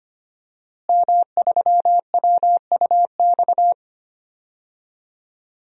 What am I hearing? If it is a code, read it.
Morse code, heard as a keyed tone, about 25 wpm, 700 Hz. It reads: M3WUX
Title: DXCC-100 Call Signs - 5 Characters - Repeated 3x - 25wpm